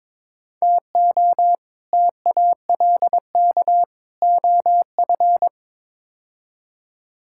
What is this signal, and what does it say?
Morse code, heard as a keyed tone, about 22 wpm, 700 Hz. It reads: TO TALK OF
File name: practice / 22wpm / Build A Phrase - 5-Word Phrases - 22wpm